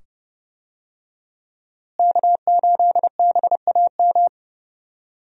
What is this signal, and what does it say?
Morse code, heard as a keyed tone, about 30 wpm, 700 Hz. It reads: K8BAM
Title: USA Call Signs - 5 Characters - Repeated 2x - 30wpm